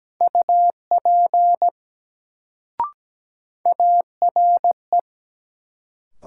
Morse code, heard as a keyed tone, about 17 wpm, 700 Hz. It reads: UP ARE
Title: Sentences from Top 100 Words - New Word Drill - 17wpm